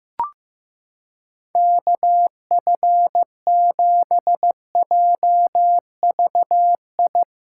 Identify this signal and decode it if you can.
Morse code, heard as a keyed tone, about 15 wpm, 700 Hz. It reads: KF7JVI
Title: USA Callsign DE Callsign - 6 Characters - 15wpm